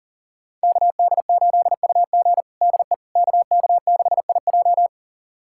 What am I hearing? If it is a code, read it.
Morse code, heard as a keyed tone, about 40 wpm, 700 Hz. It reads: KD8UG DE KK6IJ
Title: USA Callsign DE Callsign - 5 Characters - 40wpm